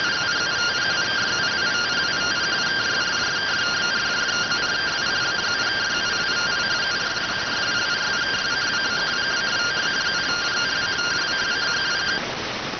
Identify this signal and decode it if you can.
Two-tone teleprinter signal, about 170 Hz shift.